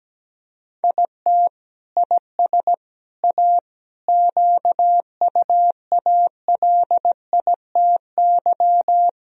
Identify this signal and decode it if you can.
Morse code, heard as a keyed tone, about 17 wpm, 700 Hz. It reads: IT IS A QUALITY
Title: Sentences from Top 1000 Words - Review Part 4 - 17wpm